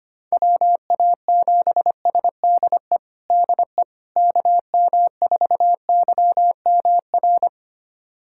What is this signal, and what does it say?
Morse code, heard as a keyed tone, about 25 wpm, 700 Hz. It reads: WA7SDE DE KM4YMR
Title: USA Callsign DE Callsign - 6 Characters - 25wpm